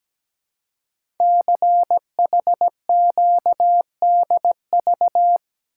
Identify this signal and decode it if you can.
Morse code, heard as a keyed tone, about 17 wpm, 700 Hz. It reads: CHQDV
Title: Code Groups - 5 Letter:Numbers - 17wpm